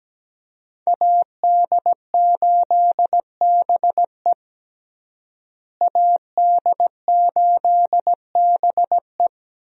Morse code, heard as a keyed tone, about 17 wpm, 700 Hz. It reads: AD8BE AD8BE
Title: USA Call Signs - 5 Characters - Repeated 2x - 17wpm